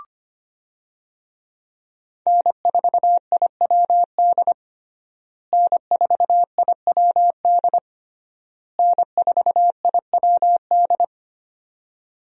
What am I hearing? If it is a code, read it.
Morse code, heard as a keyed tone, about 25 wpm, 700 Hz. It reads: N4IWD N4IWD N4IWD
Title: USA Call Signs - 5 Characters - Repeated 3x - 25wpm